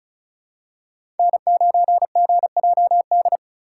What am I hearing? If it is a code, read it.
Morse code, heard as a keyed tone, about 36 wpm, 690 Hz. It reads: N9GJD
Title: USA Call Signs - 5 Characters - 3x Character Spacing - 35wpm